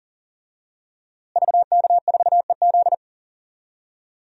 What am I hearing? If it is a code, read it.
Morse code, heard as a keyed tone, about 40 wpm, 700 Hz. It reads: VK4EZ
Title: DXCC-100 Call Signs - 5 Characters - Repeated 3x - 40wpm